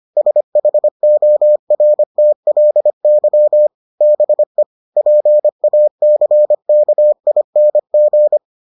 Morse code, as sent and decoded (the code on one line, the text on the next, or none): ... .... --- .-. - .-.. -.-- / -... / .--. .- -.-. -.- .. -. --.
SHORTLY BE PACKING